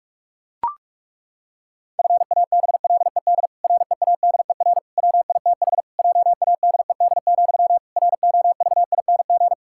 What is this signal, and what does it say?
Morse code, heard as a keyed tone, about 41 wpm, 700 Hz. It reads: FABLED READER WITH JADED, ROVING
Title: Pangram Challenge v2 - 45wpm